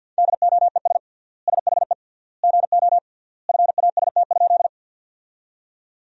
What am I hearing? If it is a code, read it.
Morse code, heard as a keyed tone, about 50 wpm, 700 Hz. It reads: DOES SHE GO FAST?
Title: 3x Word-Spacing - Sentences from Top 400 Words - 50wpm